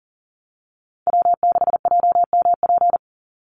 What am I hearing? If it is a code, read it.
Morse code, heard as a keyed tone, about 40 wpm, 700 Hz. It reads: W6JMP
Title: Call Signs - 40wpm